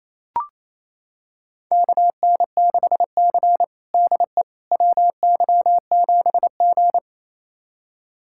Morse code, heard as a keyed tone, about 28 wpm, 700 Hz. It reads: KN6C DE WY7G
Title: USA Callsign DE Callsign - 4 Characters - 28wpm